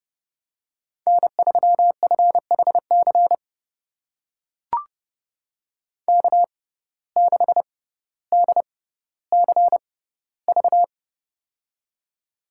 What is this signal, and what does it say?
Morse code, heard as a keyed tone, about 30 wpm, spaced as for 15 wpm, 700 Hz. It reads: N3FHC K6DCV
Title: USA Call Signs - 5 Characters - 6x Character Spacing - 30wpm